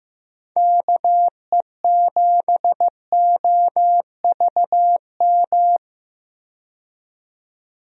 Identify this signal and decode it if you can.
Morse code, heard as a keyed tone, about 15 wpm, 700 Hz. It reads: KE7OVM
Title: USA Call Signs - 6 Characters - 15wpm